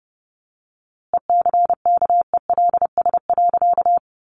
Morse code, heard as a keyed tone, about 30 wpm, 700 Hz. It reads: ECKELS.
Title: A Sound Of Thunder 30wpm